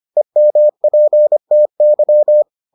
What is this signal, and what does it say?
Morse code, heard as a keyed tone, about 25 wpm, 600 Hz. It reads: EMPTY